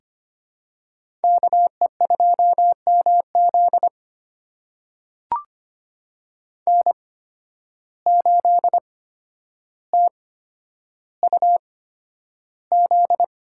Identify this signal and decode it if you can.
Morse code, heard as a keyed tone, about 25 wpm, spaced as for 11 wpm, 700 Hz. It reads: KE2MZ N8TUZ
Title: USA Call Signs - 5 Characters - 8x Character Spacing - 25wpm